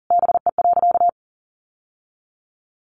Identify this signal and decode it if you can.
Morse code, heard as a keyed tone, about 40 wpm, 700 Hz. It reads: BE.
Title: Sentences from Top 700 Words 40wpm